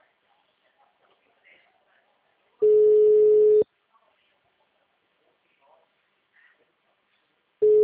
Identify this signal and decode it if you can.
Ringback tone (Europe, 425 Hz), with speech.